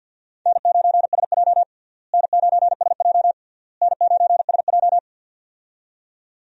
Morse code, heard as a keyed tone, about 50 wpm, 700 Hz. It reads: N9SJ N9SJ N9SJ
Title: USA Call Signs - 4 Characters - Repeated 3x - 50wpm